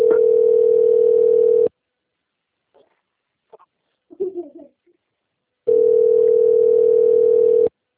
Ringback tone (North America), 440 + 480 Hz.